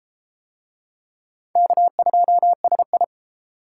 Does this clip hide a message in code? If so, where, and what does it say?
Yes: Morse code from 1.6 to 3.0 s, heard as a keyed tone, about 30 wpm, 700 Hz. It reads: K2SI